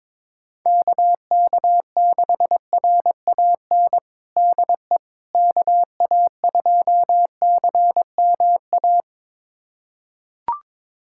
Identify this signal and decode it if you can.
Morse code, heard as a keyed tone, about 22 wpm, 700 Hz. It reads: KK6RAN DE KA2CMA
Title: USA Callsign DE Callsign - 6 Characters - 22wpm